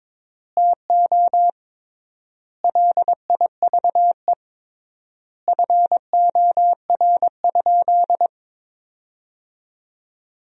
Morse code, heard as a keyed tone, about 22 wpm, 700 Hz. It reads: TO LIVE FOR?
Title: 3x Word-Spacing - Sentences from Top 200 Words - 22wpm